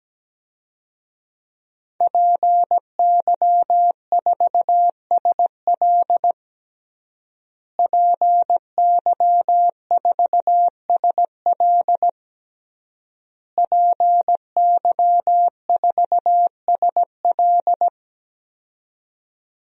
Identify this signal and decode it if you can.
Morse code, heard as a keyed tone, about 17 wpm, 700 Hz. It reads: PY4SL PY4SL PY4SL
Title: DXCC-100 Call Signs - 5 Characters - Repeated 3x - 17wpm